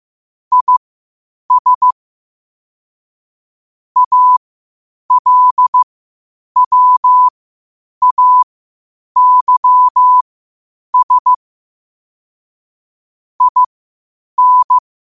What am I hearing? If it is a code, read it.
Morse code, heard as a keyed tone, about 15 wpm, spaced as for 8 wpm, 1000 Hz. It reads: IS ALWAYS IN
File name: qotdfiles / 8wpm